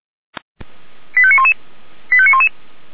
Five-tone selective call (ZVEI), heard as quick runs of five tones, tones of about 70 ms. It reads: 85911 85911